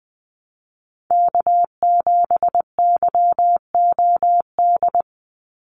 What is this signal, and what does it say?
Morse code, heard as a keyed tone, about 19 wpm, 700 Hz. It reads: K7YOD
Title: USA Call Signs 20wpm